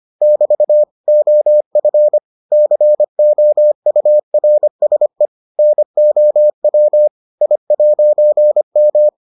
Morse code, heard as a keyed tone, about 25 wpm, 600 Hz. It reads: <BT> OF COURSE NOW I'M